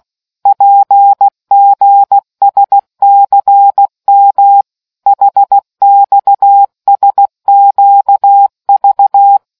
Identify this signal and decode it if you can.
Morse code, heard as a keyed tone, about 16 wpm, 800 Hz. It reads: PGSCM HXSQV